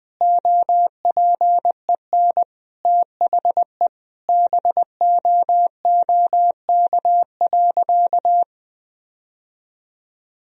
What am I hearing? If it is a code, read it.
Morse code, heard as a keyed tone, about 20 wpm, 700 Hz. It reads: OPEN THE BOOK.
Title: Sentences from Top 1000 Words - Review Part 1 - 20wpm